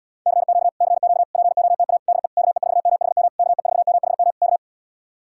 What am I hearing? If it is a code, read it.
Morse code, heard as a keyed tone, about 77 wpm, 700 Hz. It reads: CQ CQ POTA DE KE4MHM KE4MHM K